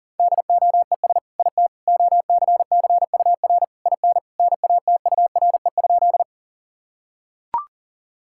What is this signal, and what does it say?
Morse code, heard as a keyed tone, about 40 wpm, 700 Hz. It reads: DOES IT OCCUR IN NATURE?
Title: Sentences from Top 1000 Words - Review Part 6 - 40wpm